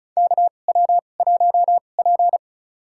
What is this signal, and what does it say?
Morse code, heard as a keyed tone, about 34 wpm, spaced as for 25 wpm, 700 Hz. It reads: KW1P
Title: USA Call Signs - 4 Characters - 2x Character Spacing - 35wpm